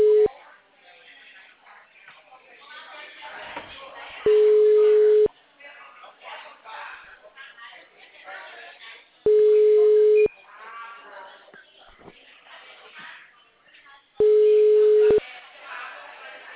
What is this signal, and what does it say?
Ringback tone (Europe, 425 Hz), with speech.